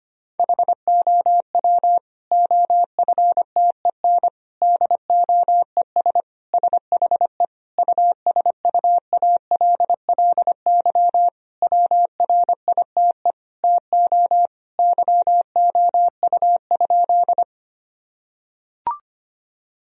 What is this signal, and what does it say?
Morse code, heard as a keyed tone, about 25 wpm, 700 Hz. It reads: HOW OFTEN DOES SHE USUALLY WRITE TO YOU?
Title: Sentences from Top 1000 Words - Review Part 3 - 25wpm